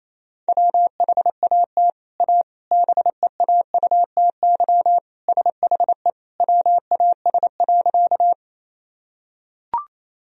Morse code, heard as a keyed tone, about 28 wpm, 700 Hz. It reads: WHAT A BEAUTY SHE WAS.
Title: Sentences from Top 1000 Words - Review Part 2 - 28wpm